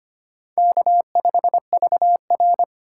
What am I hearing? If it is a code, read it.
Morse code, heard as a keyed tone, about 25 wpm, 700 Hz. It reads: K5VR
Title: K1USN SST QSO Practice - 25wpm